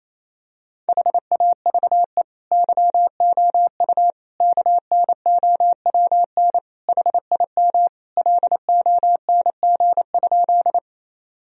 Morse code, heard as a keyed tone, about 28 wpm, 700 Hz. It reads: HAVE YOU KNOWN HIM LONG?
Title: Sentences from Top 1000 Words - Review Part 3 - 28wpm